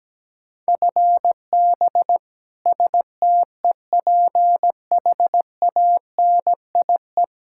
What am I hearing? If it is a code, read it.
Morse code, heard as a keyed tone, about 17 wpm, 700 Hz. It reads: FB STEPHANIE
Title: Practice QSO - Exchange 3 - More Info - 17wpm